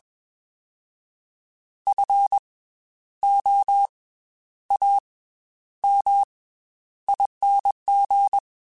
Morse code, heard as a keyed tone, about 21 wpm, 800 Hz. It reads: F O A M ING